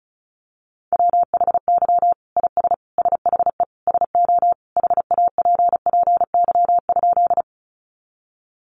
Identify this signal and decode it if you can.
Morse code, heard as a keyed tone, about 34 wpm, 700 Hz. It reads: WHY IS SHE SO HAPPY?
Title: Sentences from Top 1000 Words 35wpm